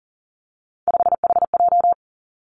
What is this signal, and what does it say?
Morse code, heard as a keyed tone, about 42 wpm, 700 Hz. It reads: 5HJ
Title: Three Challenging Letter-Number Combinations 40wpm